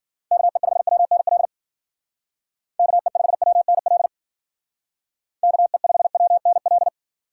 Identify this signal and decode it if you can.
Morse code, heard as a keyed tone, about 50 wpm, 700 Hz. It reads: KE5WNL KE5WNL KE5WNL
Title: Speed Racing - USA Call Signs 40wpm